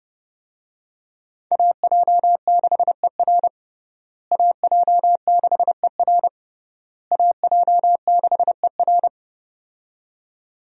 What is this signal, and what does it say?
Morse code, heard as a keyed tone, about 30 wpm, 700 Hz. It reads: AJ6ER AJ6ER AJ6ER
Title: USA Call Signs - 5 Characters - Repeated 3x - 30wpm